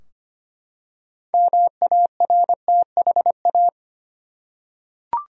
Morse code, heard as a keyed tone, about 25 wpm, 700 Hz. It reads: MARTHA
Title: USA Common Names - 25wpm